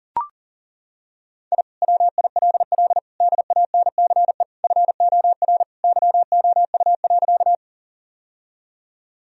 Morse code, heard as a keyed tone, about 37 wpm, 700 Hz. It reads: I WILL DANCE FOR YOU.
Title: Sentences from Top 1000 Words - Review Part 4 - 40wpm